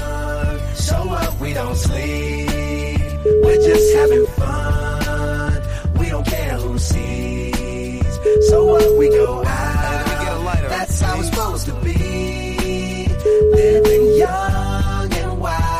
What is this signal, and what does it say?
Ringback tone (Europe, 425 Hz), with speech.